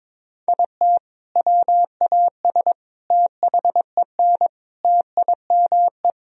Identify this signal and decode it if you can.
Morse code, heard as a keyed tone, about 22 wpm, 700 Hz. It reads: IT WAS THEN TIME
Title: Sentences from Top 1000 Words - Review Part 1 - 22wpm